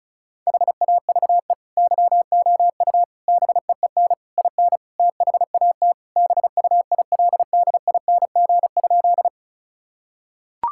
Morse code, heard as a keyed tone, about 34 wpm, 700 Hz. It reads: HAVE YOU BEEN IN THAT BUILDING?
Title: Sentences from Top 1000 Words - Review Part 5 - 35wpm